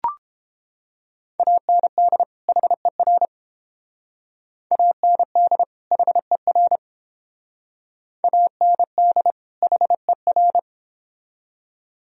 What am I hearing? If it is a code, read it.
Morse code, heard as a keyed tone, about 28 wpm, 700 Hz. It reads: AND HER AND HER AND HER